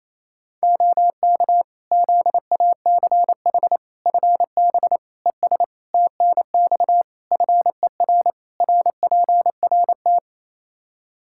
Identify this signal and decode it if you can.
Morse code, heard as a keyed tone, about 27 wpm, 700 Hz. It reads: OK ZACH FB ES TNX FER RPRT
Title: K1USN SST QSO Practice 28wpm